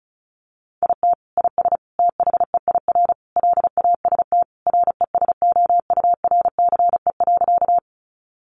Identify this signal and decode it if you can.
Morse code, heard as a keyed tone, about 34 wpm, 700 Hz. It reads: IT IS THEIR LAST RESOURCE.
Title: Sentences from Top 1300 Words 35wpm